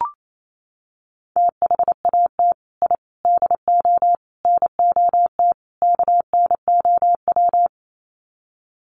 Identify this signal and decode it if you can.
Morse code, heard as a keyed tone, about 28 wpm, 700 Hz. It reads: THAT I DO NOT KNOW
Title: Sets of 5 Words Encore2 28wpm